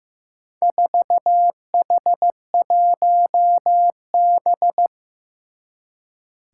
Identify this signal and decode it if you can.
Morse code, heard as a keyed tone, about 15 wpm, 700 Hz. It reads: 4H1B
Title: Four Challenging Letter-Number Combinations 15wpm